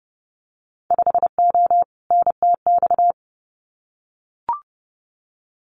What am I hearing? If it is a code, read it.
Morse code, heard as a keyed tone, about 30 wpm, 700 Hz. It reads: 5O NTX
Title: Winter Field Day Exchanges Encore 30wpm